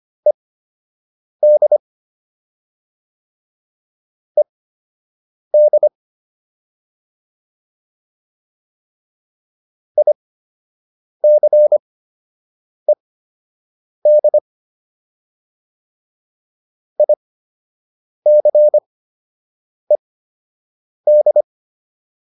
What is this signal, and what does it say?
Morse code, heard as a keyed tone, about 25 wpm, spaced as for 7 wpm, 600 Hz. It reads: ED ED ICED ICED